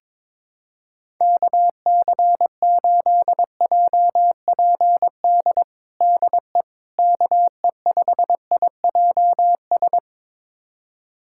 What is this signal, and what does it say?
Morse code, heard as a keyed tone, about 21 wpm, 700 Hz. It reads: KC8JPD DE KE5IJS
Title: USA Callsign DE Callsign - 6 Characters - 22wpm